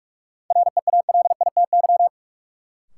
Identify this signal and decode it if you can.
Morse code, heard as a keyed tone, about 46 wpm, 700 Hz. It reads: REALITY
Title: Sentences from Top 1200 Words - New Word Drill Alt - 45wpm